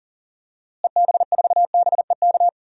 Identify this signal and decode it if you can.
Morse code, heard as a keyed tone, about 37 wpm, 700 Hz. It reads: EB4BEK
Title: DXCC-100 Call Signs - 6 Characters - Repeated 3x - 40wpm